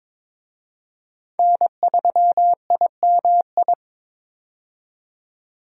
Morse code, heard as a keyed tone, about 22 wpm, 700 Hz. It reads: N3IMI
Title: USA Call Signs - 5 Characters - 22wpm